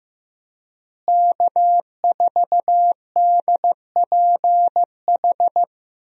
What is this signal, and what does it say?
Morse code, heard as a keyed tone, about 15 wpm, 700 Hz. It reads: K4DPH
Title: USA Call Signs - 5 Characters - Repeated 3x - 15wpm